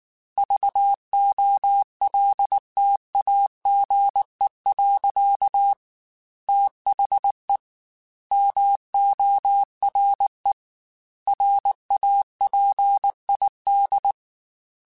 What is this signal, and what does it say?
Morse code, heard as a keyed tone, about 19 wpm, 800 Hz. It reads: VOLTAGE. THE MORE RAPID
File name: Technical Subjects / Technical Subjects-mp3s / XC @18